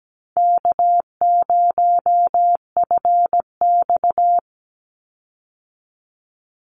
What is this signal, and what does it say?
Morse code, heard as a keyed tone, about 17 wpm, 700 Hz. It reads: K0FX
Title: Easy Call Signs Repeated 2x - 17wpm